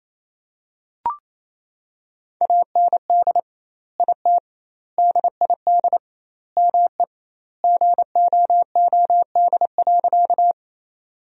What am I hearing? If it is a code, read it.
Morse code, heard as a keyed tone, about 28 wpm, 700 Hz. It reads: AND IT DID ME GOOD.
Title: 2x Word-Spacing - Sentences from Top 200 Words - 28wpm